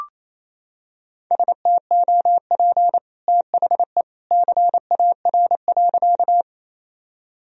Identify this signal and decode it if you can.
Morse code, heard as a keyed tone, about 28 wpm, 700 Hz. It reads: STOP THE CAR.
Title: Sentences from Top 1000 Words - Review Part 1 - 28wpm